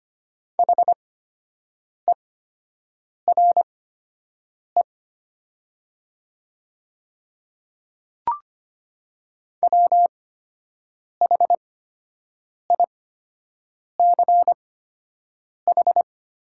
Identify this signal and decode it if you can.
Morse code, heard as a keyed tone, about 25 wpm, spaced as for 7 wpm, 700 Hz. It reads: HERE WHICH
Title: 8x Character Spacing - Top 200 Words - 25wpm